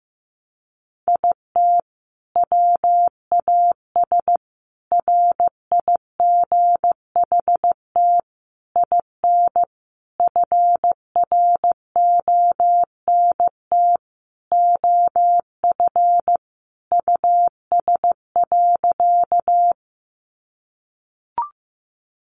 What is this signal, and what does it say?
Morse code, heard as a keyed tone, about 15 wpm, 700 Hz. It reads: IT WAS RIGHT IN FRONT OF US.
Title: Sentences from Top 500 Words 15wpm